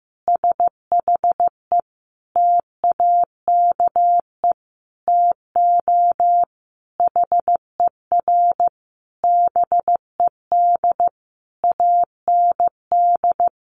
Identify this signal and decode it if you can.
Morse code, heard as a keyed tone, about 15 wpm, 700 Hz. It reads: SHE TAKE TO HER BED AND
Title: Sentences from Top 1200 Words 15wpm